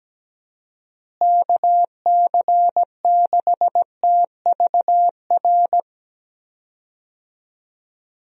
Morse code, heard as a keyed tone, about 17 wpm, 700 Hz. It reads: KC6TVR
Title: USA Call Signs - 6 Characters - Repeated 3x - 17wpm